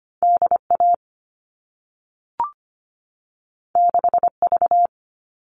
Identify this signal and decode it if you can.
Morse code, heard as a keyed tone, about 25 wpm, 700 Hz. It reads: DA 6V
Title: All Cut Numbers - 2-Digit 25wpm